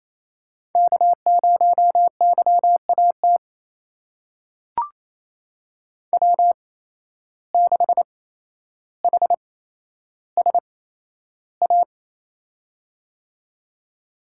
Morse code, heard as a keyed tone, about 28 wpm, spaced as for 12 wpm, 700 Hz. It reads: K0YAT W6HSA
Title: USA Call Signs - 5 Characters - 8x Character Spacing - 28wpm